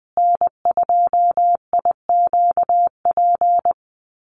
Morse code, heard as a keyed tone, about 20 wpm, 700 Hz. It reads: N2IQP
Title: USA Call Signs Repeated 3x 20wpm